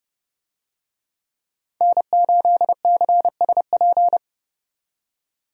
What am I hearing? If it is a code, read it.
Morse code, heard as a keyed tone, about 31 wpm, 700 Hz. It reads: N8CSP
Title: USA Call Signs - 5 Characters - 30wpm